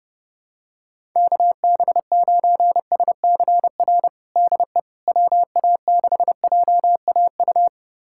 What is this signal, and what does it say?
Morse code, heard as a keyed tone, about 30 wpm, 700 Hz. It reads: KB9SCR DE WA6JAU
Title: USA Callsign DE Callsign - 6 Characters - 30wpm